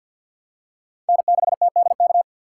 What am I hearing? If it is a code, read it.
Morse code, heard as a keyed tone, about 46 wpm, 700 Hz. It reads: N6TDK
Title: USA Call Signs - 5 Characters - 3x Character Spacing - 50wpm